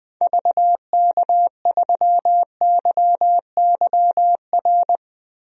Morse code, heard as a keyed tone, about 20 wpm, 700 Hz. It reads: VK3YYR
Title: DXCC-100 Call Signs - 6 Characters - Repeated 2x - 20wpm